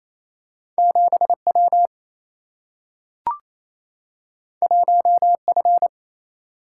Morse code, heard as a keyed tone, about 28 wpm, 700 Hz. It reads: 7W 1F
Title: Single-Letter Single-Number 28wpm